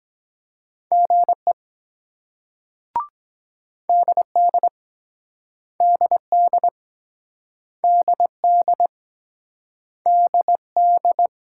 Morse code, heard as a keyed tone, about 23 wpm, 700 Hz. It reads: GE DD DD DD DD